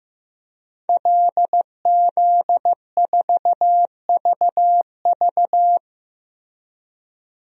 Morse code, heard as a keyed tone, about 15 wpm, 700 Hz. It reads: LZ4VV
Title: DXCC-100 Call Signs - 5 Characters - Repeated 2x - 15wpm